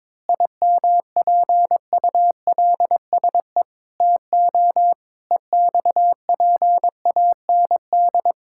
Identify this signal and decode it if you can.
Morse code, heard as a keyed tone, about 22 wpm, 700 Hz. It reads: IMPULSE TO EXPAND
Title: Pangram Challenge v2 - 22wpm